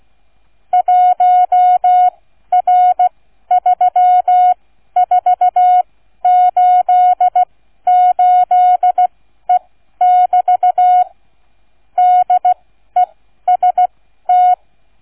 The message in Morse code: 1R3488E<BT> DEST

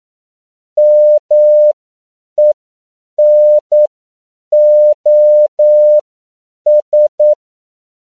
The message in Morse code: MENOS

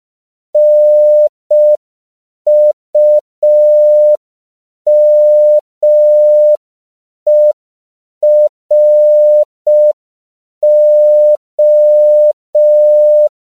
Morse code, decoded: NUMERO

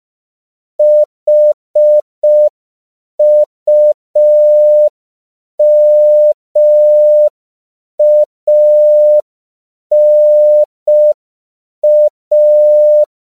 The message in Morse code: HUMANA

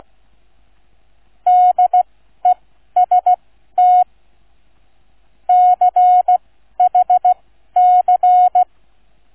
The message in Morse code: DEST CHC

